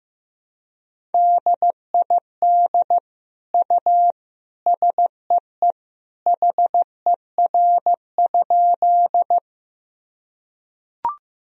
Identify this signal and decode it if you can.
Morse code, heard as a keyed tone, about 15 wpm, 700 Hz. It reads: DID U SEE HER?
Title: Sentences from Top 100 Words and Abbreviations 15wpm